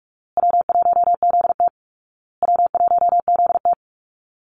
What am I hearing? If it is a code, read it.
Morse code, heard as a keyed tone, about 44 wpm, 700 Hz. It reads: W1ZT W1ZT
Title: Call Signs Repeated 2x - 45wpm